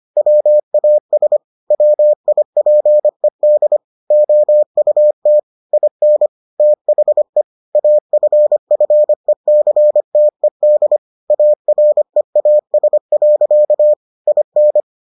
Morse code, heard as a keyed tone, about 25 wpm, 600 Hz. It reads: WAS WIPED OUT IN THE AFFECTED AREAS. IN